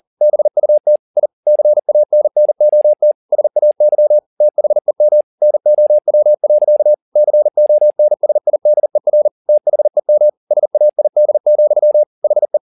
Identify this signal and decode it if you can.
Morse code, heard as a keyed tone, about 40 wpm, 600 Hz. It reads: BUT I CANNOT SAY THEM NOW. CONSIDER THEM SAID, HE
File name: the old man and the sea 2 40wpm